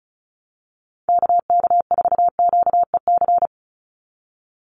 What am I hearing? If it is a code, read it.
Morse code, heard as a keyed tone, about 34 wpm, 700 Hz. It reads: KK4QEC
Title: USA Call Signs 35wpm